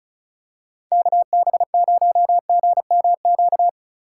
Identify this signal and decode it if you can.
Morse code, heard as a keyed tone, about 34 wpm, 700 Hz. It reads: KB0GMQ
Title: USA Call Signs - 4x Character Spacing - 35wpm